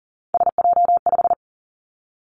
Morse code, heard as a keyed tone, about 42 wpm, 700 Hz. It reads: SJ5
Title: Three Challenging Letter-Number Combinations 40wpm